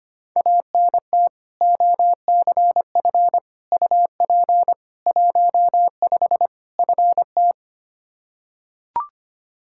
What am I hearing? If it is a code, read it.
Morse code, heard as a keyed tone, about 25 wpm, 700 Hz. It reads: ANT OCF UP 15 FT
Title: Practice QSO - Exchange 2 - Info - 25wpm